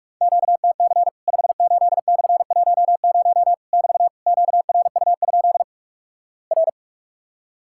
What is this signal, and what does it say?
Morse code, heard as a keyed tone, about 44 wpm, 700 Hz. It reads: QTC 58/10 <BT> QRU? R
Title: Worked All Europe DX Contest - QTC Exchange - 45wpm